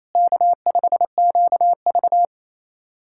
Morse code, heard as a keyed tone, about 28 wpm, 700 Hz. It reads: K5QV
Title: SOTA QSO Practice 28wpm